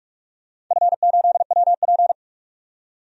Morse code, heard as a keyed tone, about 47 wpm, 700 Hz. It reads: F8WP
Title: DXCC-100 Call Signs - 4 Characters - 45wpm